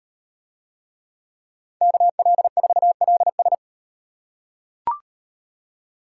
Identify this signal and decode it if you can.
Morse code, heard as a keyed tone, about 38 wpm, 700 Hz. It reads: KL4LS